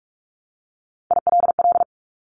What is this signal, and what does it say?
Morse code, heard as a keyed tone, about 46 wpm, 700 Hz. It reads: ILL
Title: Top Three Letter Combinations in Words 45wpm